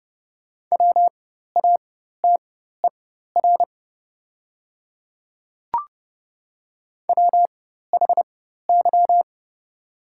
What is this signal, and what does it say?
Morse code, heard as a keyed tone, about 30 wpm, spaced as for 14 wpm, 700 Hz. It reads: WATER WHY
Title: 4x Character Spacing - Top 200 Words - 30wpm